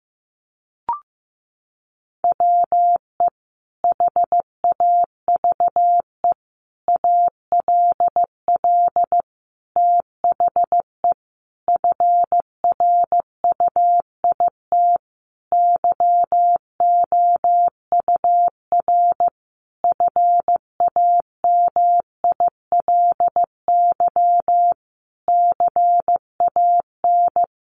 Morse code, heard as a keyed tone, about 15 wpm, 700 Hz. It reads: WE HAVE ALL THE FRUIT YOUR FAMILY CAN
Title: Sentences from Top 1300 Words 15wpm